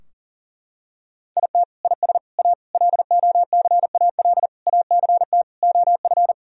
Morse code, heard as a keyed tone, about 40 wpm, 700 Hz. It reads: IT IS A LOCAL ACT OF
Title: Sentences from Top 1000 Words - Review Part 3 - 40wpm